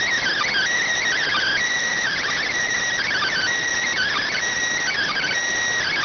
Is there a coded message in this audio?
Yes: Emergency Alert System (SAME) tones (header bursts) from 1.1 to 5.3 s, heard as harsh buzzing bursts.